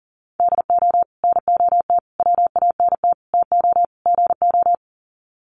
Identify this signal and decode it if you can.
Morse code, heard as a keyed tone, about 40 wpm, 700 Hz. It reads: DO NOT WANT TO GO
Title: Sets of 5 Words 40wpm